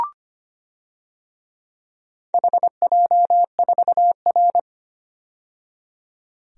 Morse code, heard as a keyed tone, about 25 wpm, 700 Hz. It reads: HJ4R
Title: DXCC-100 Call Signs - 4 Characters - 25wpm